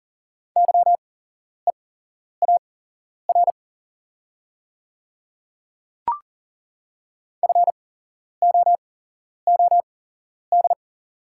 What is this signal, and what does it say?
Morse code, heard as a keyed tone, about 40 wpm, spaced as for 11 wpm, 690 Hz. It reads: YEAR FOOD
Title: 8x Character Spacing - Top 300 Words - 40wpm